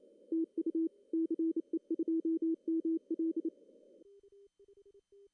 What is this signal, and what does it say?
Morse code, heard as a keyed tone, about 27 wpm, 330 Hz. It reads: TU CE2ML
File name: CW / cqww-cw-k5tr